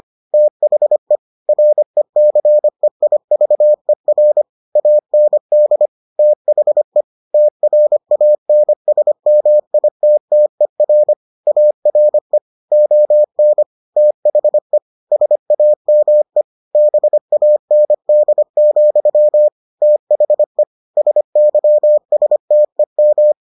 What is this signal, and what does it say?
Morse code, heard as a keyed tone, about 25 wpm, 600 Hz. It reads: THE RECEIVER AND THE TRANSMITTER ARE ON THE SAME BAND, THE SYSTEM